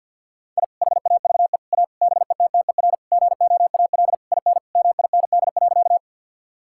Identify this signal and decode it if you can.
Morse code, heard as a keyed tone, about 50 wpm, 700 Hz. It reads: I HAVE A BETTER GOAL IN MIND.
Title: Sentences from Top 1000 Words - Review Part 5 - 50wpm